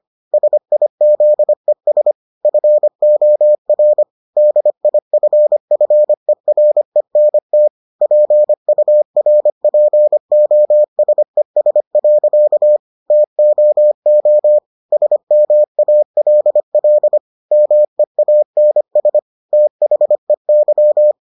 Morse code, heard as a keyed tone, about 25 wpm, 600 Hz. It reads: SIZES FOR DIFFERENT PURPOSES. TOO SMALL MEANS THEY